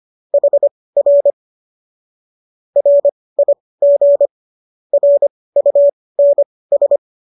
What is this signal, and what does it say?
Morse code, heard as a keyed tone, about 25 wpm, spaced as for 18 wpm, 570 Hz. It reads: HR RIG RUNS
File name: qso218 18